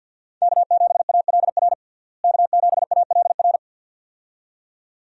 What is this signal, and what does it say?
Morse code, heard as a keyed tone, about 48 wpm, 700 Hz. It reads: K7ALR K7ALR
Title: USA Call Signs - 5 Characters - Repeated 2x - 50wpm